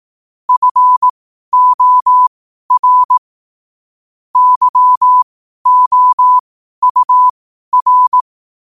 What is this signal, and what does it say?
Morse code, heard as a keyed tone, about 18 wpm, spaced as for 12 wpm, 1000 Hz. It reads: FOR YOUR
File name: qotdfiles / 12wpm